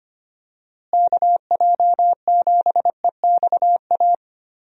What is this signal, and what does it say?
Morse code, heard as a keyed tone, about 25 wpm, 700 Hz. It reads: KJ7EXA
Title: USA Call Signs - 6 Characters - Repeated 2x - 25wpm